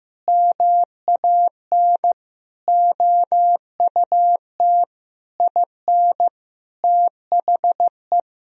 Morse code, heard as a keyed tone, about 16 wpm, 700 Hz. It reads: MAN OUT IN THE
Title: Sentences from Top 1000 Words - Review Part 2 - 15wpm